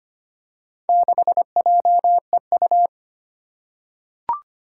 Morse code, heard as a keyed tone, about 25 wpm, 700 Hz. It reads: 6JEU
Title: Four Challenging Letter-Number Combinations 25wpm